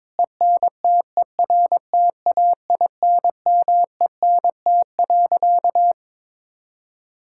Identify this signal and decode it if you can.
Morse code, heard as a keyed tone, about 22 wpm, 700 Hz. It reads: ENTERTAINMENT.
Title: Sentences from Top 1000 Words - Review Part 6 - 22wpm